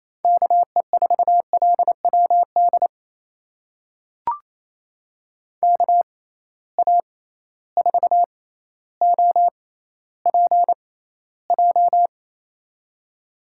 Morse code, heard as a keyed tone, about 28 wpm, spaced as for 14 wpm, 700 Hz. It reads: KE4LWD KA4OPJ
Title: USA Call Signs - 6 Characters - 6x Character Spacing - 28wpm